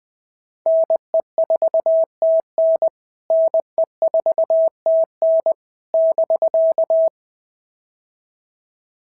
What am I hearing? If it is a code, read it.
Morse code, heard as a keyed tone, about 20 wpm, 650 Hz. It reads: NE4TN NE4TN <BK>
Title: POTA QSO Practice - 20wpm